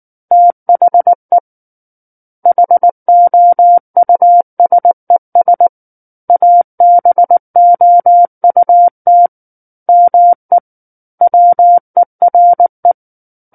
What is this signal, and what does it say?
Morse code, heard as a keyed tone, about 19 wpm, 700 Hz. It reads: THE HOUSES ABOUT ME WERE
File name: Book One / 10-Chapter 10 - 19 WPM